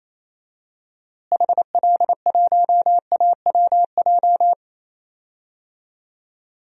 Morse code, heard as a keyed tone, about 28 wpm, 700 Hz. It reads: HL1AWJ
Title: DXCC-100 Call Signs - 6 Characters - Repeated 2x - 28wpm